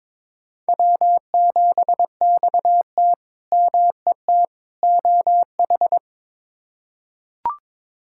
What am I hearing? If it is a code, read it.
Morse code, heard as a keyed tone, about 22 wpm, 700 Hz. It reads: W7XT MET OH
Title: K1USN SST QSO Practice - 22wpm